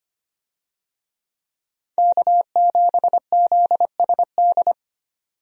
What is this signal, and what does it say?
Morse code, heard as a keyed tone, about 25 wpm, 700 Hz. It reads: K7ZSD
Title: USA Call Signs - 5 Characters - 25wpm